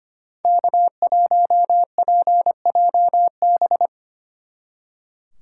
Morse code, heard as a keyed tone, about 30 wpm, 700 Hz. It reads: K1PJB
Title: Speed Racing - USA 1x3 Call Signs 25wpm